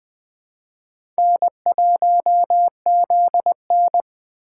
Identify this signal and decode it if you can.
Morse code, heard as a keyed tone, about 21 wpm, 700 Hz. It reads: N1ZN